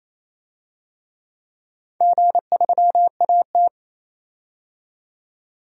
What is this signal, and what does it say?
Morse code, heard as a keyed tone, about 28 wpm, 700 Hz. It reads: G3AT
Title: DXCC-100 Call Signs - 4 Characters - 28wpm